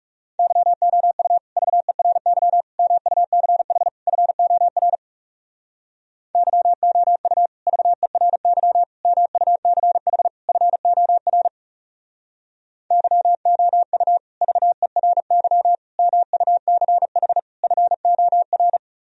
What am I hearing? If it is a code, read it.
Morse code, heard as a keyed tone, about 40 wpm, 700 Hz. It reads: YOU VERY MUCH FOR YOU VERY MUCH FOR YOU VERY MUCH FOR